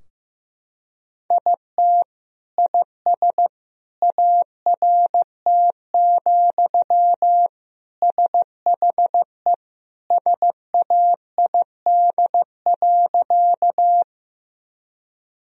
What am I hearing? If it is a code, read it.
Morse code, heard as a keyed tone, about 16 wpm, 700 Hz. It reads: IT IS ART, SHE SAID.
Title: Sentences from Top 1000 Words - Review Part 3 - 15wpm